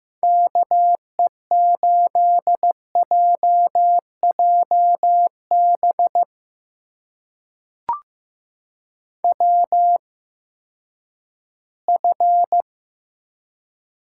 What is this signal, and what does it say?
Morse code, heard as a keyed tone, about 15 wpm, spaced as for 10 wpm, 700 Hz. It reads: KE8JJB WF
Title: USA Call Signs - 8x Character Spacing - 15wpm